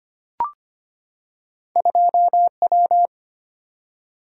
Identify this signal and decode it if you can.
Morse code, heard as a keyed tone, about 25 wpm, 690 Hz. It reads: 2W
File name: Two Letter-Number Combinations - 25wpm